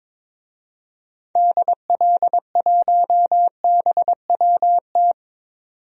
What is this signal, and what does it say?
Morse code, heard as a keyed tone, about 22 wpm, 700 Hz. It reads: DL1BWT
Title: DXCC-100 Call Signs - 6 Characters - 22wpm